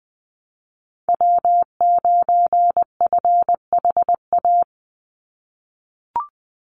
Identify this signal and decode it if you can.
Morse code, heard as a keyed tone, about 19 wpm, 700 Hz. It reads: W9FHA
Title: NA SOTA Chaser Callsigns 20wpm